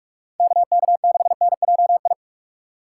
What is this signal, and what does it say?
Morse code, heard as a keyed tone, about 60 wpm, spaced as for 52 wpm, 700 Hz. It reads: KK6NJI
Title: Speed Racing - USA 2x3 Call Signs 45wpm